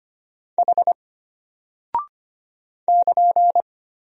Morse code, heard as a keyed tone, about 25 wpm, 700 Hz. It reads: H <KN>
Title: Single Character (Letter:Number:Punctuation:Prosign) Alt 25wpm